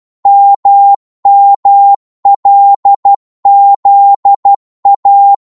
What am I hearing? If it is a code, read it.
Morse code, heard as a keyed tone, about 12 wpm, 800 Hz. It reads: MMLZA
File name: Morse / ltr-12wpm